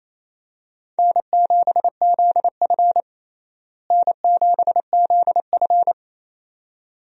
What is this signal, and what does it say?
Morse code, heard as a keyed tone, about 28 wpm, 700 Hz. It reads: N7ZF N7ZF
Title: USA Call Signs - 4 Characters - Repeated 2x - 28wpm